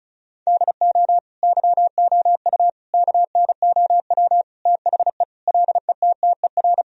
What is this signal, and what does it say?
Morse code, heard as a keyed tone, about 36 wpm, 700 Hz. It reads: DO YOU KNOW THE LETTER
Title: Sentences from Top 1000 Words - Review Part 2 - 35wpm